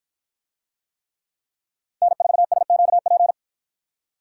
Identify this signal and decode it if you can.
Morse code, heard as a keyed tone, about 51 wpm, 700 Hz. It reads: N4SQP